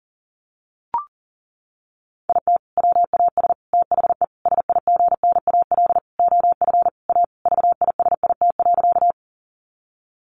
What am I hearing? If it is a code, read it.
Morse code, heard as a keyed tone, about 37 wpm, 700 Hz. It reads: IT WAS THE SIGNAL OF A VISIT.
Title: Sentences from Top 1400 Words 40wpm